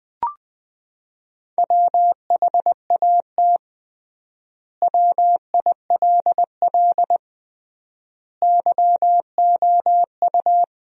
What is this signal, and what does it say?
Morse code, heard as a keyed tone, about 20 wpm, 700 Hz. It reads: WHAT WILL YOU
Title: 3x Word-Spacing - Sentences from Top 200 Words - 20wpm